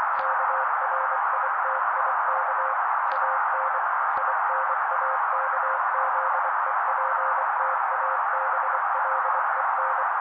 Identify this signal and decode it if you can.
Morse code, heard as a keyed tone, about 24 wpm, 530 Hz. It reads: KRITIK AN INAKZEPTABLEN
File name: daily / 23wpm noise